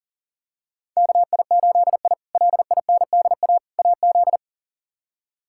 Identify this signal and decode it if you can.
Morse code, heard as a keyed tone, about 40 wpm, 700 Hz. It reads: KI8I LINDA AZ
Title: K1USN SST QSO Practice - 40wpm